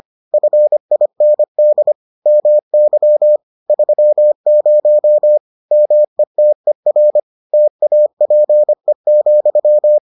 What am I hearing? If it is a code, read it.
Morse code, heard as a keyed tone, about 25 wpm, 600 Hz. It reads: FIND MY 30 METER TAPE,